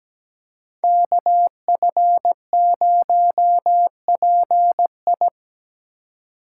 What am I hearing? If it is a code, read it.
Morse code, heard as a keyed tone, about 17 wpm, 700 Hz. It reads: KF0PI
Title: USA Call Signs - 5 Characters - 3x Character Spacing - 17wpm